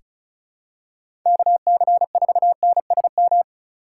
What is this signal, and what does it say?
Morse code, heard as a keyed tone, about 36 wpm, 700 Hz. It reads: KC4NSM
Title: USA Call Signs - 6 Characters - 3x Character Spacing - 35wpm